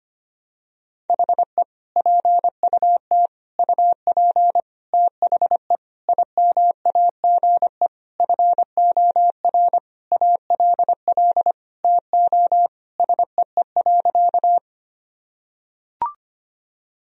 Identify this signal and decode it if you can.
Morse code, heard as a keyed tone, about 25 wpm, 700 Hz. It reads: HE PUT UP THE IMAGE FOR ALL TO SEE.
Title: Sentences from Top 1000 Words - Review Part 4 - 25wpm